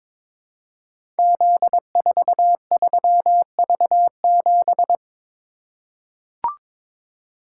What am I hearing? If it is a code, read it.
Morse code, heard as a keyed tone, about 22 wpm, 700 Hz. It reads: Z43V7
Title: Code Groups - 5 Letter:Numbers - 22wpm